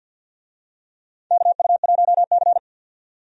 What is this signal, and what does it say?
Morse code, heard as a keyed tone, about 53 wpm, 700 Hz. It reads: KU1C
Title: USA Call Signs - 4 Characters - 50wpm